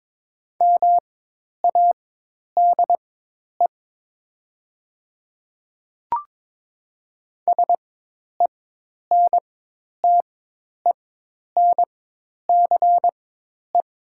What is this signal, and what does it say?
Morse code, heard as a keyed tone, about 22 wpm, spaced as for 10 wpm, 700 Hz. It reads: MADE SENTENCE
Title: 4x Character Spacing - Top 500 Words - 22wpm